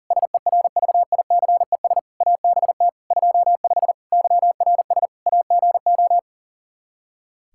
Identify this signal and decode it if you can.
Morse code, heard as a keyed tone, about 40 wpm, 700 Hz. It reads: SERVICES ABT 25 YRS AGO
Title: Practice QSO - Exchange 3 - More Info - 40wpm